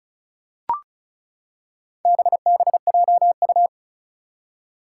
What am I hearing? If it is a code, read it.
Morse code, heard as a keyed tone, about 36 wpm, 700 Hz. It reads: BBJU
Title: Four Challenging Letter-Number Combinations 35wpm